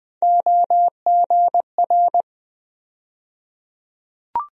Morse code, heard as a keyed tone, about 20 wpm, 700 Hz. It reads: OGR
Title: USA Call Sign Postfixes - 3 Letters 20wpm